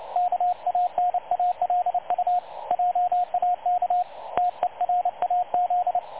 Morse code, heard as a keyed tone, about 29 wpm, 720 Hz, with radio noise. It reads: KANALU JAK TERAZ
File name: dlazaawansowanych / nagraniazpasma / websdr recording 3565.0kHz